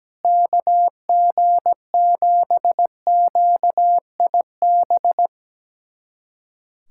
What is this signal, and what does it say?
Morse code, heard as a keyed tone, about 17 wpm, 700 Hz. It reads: KG7QIB
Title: USA Callsign DE Callsign - 6 Characters - 17wpm